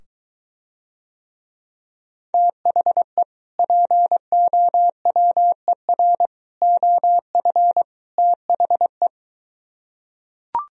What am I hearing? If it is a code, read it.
Morse code, heard as a keyed tone, about 23 wpm, 700 Hz. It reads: THE POWER OF THE